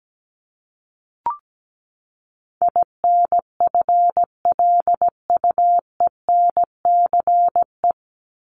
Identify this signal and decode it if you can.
Morse code, heard as a keyed tone, about 17 wpm, 700 Hz. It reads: INFLUENCE
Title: Top 1001-1250 Words 17wpm